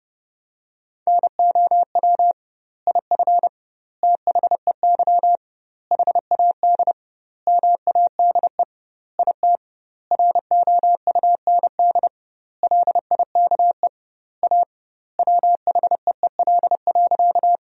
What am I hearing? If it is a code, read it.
Morse code, heard as a keyed tone, about 30 wpm, 700 Hz. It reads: NOW IF THEY HAD MADE IT ROUND LIKE A WHEEL.
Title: 2x Word-Spacing - Sentences from Top 500 Words - 30wpm